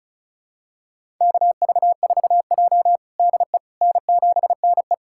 Morse code, heard as a keyed tone, about 36 wpm, 700 Hz. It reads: KV4J DE N7NE
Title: USA Callsign DE Callsign - 4 Characters - 35wpm